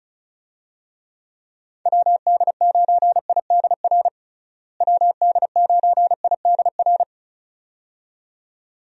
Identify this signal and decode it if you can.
Morse code, heard as a keyed tone, about 34 wpm, 700 Hz. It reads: WD9IDR WD9IDR
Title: USA Call Signs - 6 Characters - Repeated 2x - 35wpm